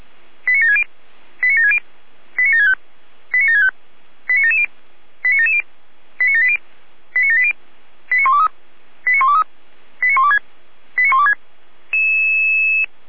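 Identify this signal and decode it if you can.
Five-tone selective call (ZVEI), heard as quick runs of five tones, tones of about 70 ms. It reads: 89760 89760 89765 89765 89770 89770 89780 89780 89123 89123 89126 89126
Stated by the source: ZVEI five-tone selective call received by SDR.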